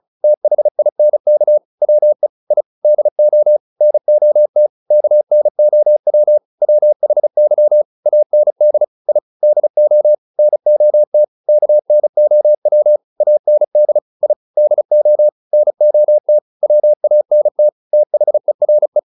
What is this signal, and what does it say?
Morse code, heard as a keyed tone, about 34 wpm, 600 Hz. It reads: THINK WE I DO NOT KNOW WHY AND I DO NOT KNOW AND I DO NOT WANT THERE